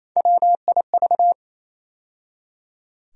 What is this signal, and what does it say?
Morse code, heard as a keyed tone, about 28 wpm, 700 Hz. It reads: WIV
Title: USA Call Sign Postfixes - 3 Letters 28wpm